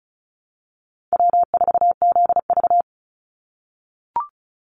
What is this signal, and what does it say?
Morse code, heard as a keyed tone, about 34 wpm, 700 Hz. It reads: W4ZV
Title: NA SOTA Chaser Callsigns 35wpm